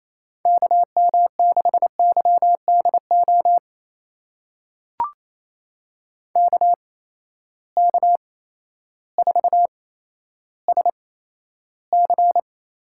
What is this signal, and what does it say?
Morse code, heard as a keyed tone, about 27 wpm, 700 Hz. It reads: K M 6 Y D O K K 4 S C
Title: USA Call Signs - 6 Characters - 8x Character Spacing - 28wpm